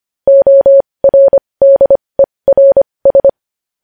Morse code, heard as a keyed tone, about 25 wpm, spaced as for 20 wpm, 550 Hz. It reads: ORDERS
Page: news headlines in morse code at 20 wpm